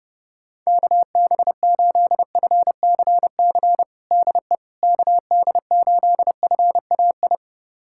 Morse code, heard as a keyed tone, about 30 wpm, 700 Hz. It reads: KB8FCC DE KD8FAI
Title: USA Callsign DE Callsign 30wpm